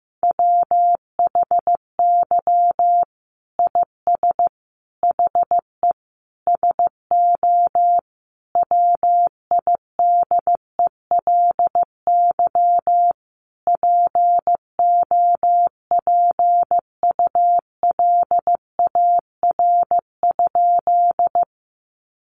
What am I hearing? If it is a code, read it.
Morse code, heard as a keyed tone, about 15 wpm, 700 Hz. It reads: WHY IS HE SO WIDELY POPULAR?
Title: Sentences from Top 1200 Words 15wpm